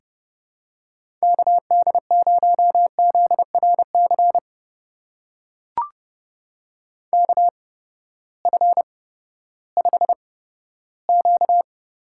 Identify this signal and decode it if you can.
Morse code, heard as a keyed tone, about 30 wpm, spaced as for 15 wpm, 700 Hz. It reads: KD0ZRC KF5Q
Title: USA Call Signs - 6 Characters - 8x Character Spacing - 30wpm